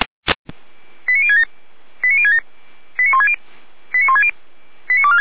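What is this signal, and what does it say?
Five-tone selective call (ZVEI), heard as quick runs of five tones, tones of about 70 ms. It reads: 89067 89067 89160 89160 89246
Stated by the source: ZVEI five-tone selective call 89072, received by SDR.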